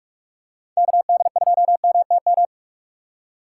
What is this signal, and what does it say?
Morse code, heard as a keyed tone, about 46 wpm, 700 Hz. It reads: KD2MTM
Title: USA Call Signs - 2x Character Spacing - 45wpm